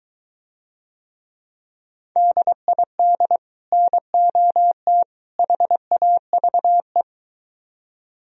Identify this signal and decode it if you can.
Morse code, heard as a keyed tone, about 23 wpm, 700 Hz. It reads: DID NOT HAVE